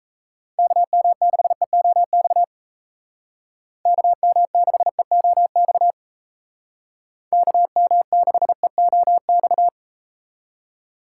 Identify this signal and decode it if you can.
Morse code, heard as a keyed tone, about 37 wpm, 700 Hz. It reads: KM6EOX KM6EOX KM6EOX